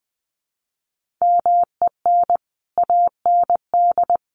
Morse code, heard as a keyed tone, about 20 wpm, 700 Hz. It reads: MEN AND
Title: Sets of 3 Words 20wpm